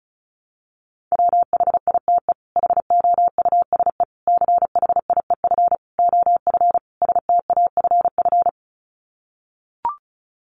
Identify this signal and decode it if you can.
Morse code, heard as a keyed tone, about 34 wpm, 700 Hz. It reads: WHITE HOUSE CHIEF OF STAFF
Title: Sets of 5 Words Encore 35wpm